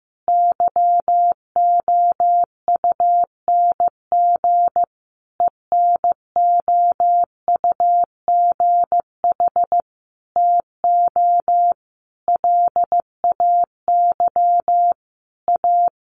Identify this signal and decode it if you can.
Morse code, heard as a keyed tone, about 15 wpm, 700 Hz. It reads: YOUNG ENOUGH TO LAY A
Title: Sentences from Top 1200 Words 15wpm